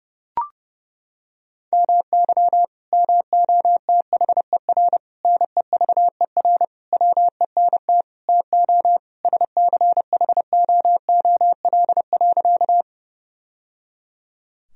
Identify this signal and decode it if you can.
Morse code, heard as a keyed tone, about 30 wpm, 700 Hz. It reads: MY MOTHER NEVER WENT TO SCHOOL.
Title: Sentences from Top 1000 Words - Review Part 1 - 30wpm